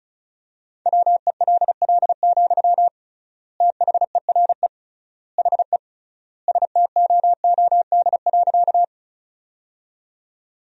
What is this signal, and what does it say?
Morse code, heard as a keyed tone, about 34 wpm, 700 Hz. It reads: WELL, THERE HE STOOD.
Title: 3x Word-Spacing - Sentences from Top 500 Words - 35wpm